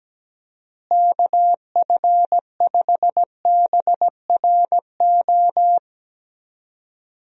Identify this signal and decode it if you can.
Morse code, heard as a keyed tone, about 17 wpm, 700 Hz. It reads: KF5BRO
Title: USA Call Signs - 6 Characters - Repeated 3x - 17wpm